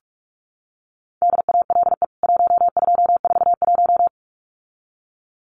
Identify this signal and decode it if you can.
Morse code, heard as a keyed tone, about 44 wpm, 700 Hz. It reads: DALE 1241
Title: CWT Exchange 45wpm